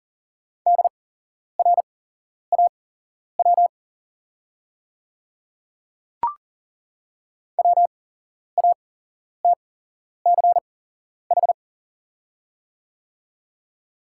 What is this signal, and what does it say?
Morse code, heard as a keyed tone, about 40 wpm, spaced as for 11 wpm, 700 Hz. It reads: DRAW WATCH
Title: 8x Character Spacing - Top 300 Words - 40wpm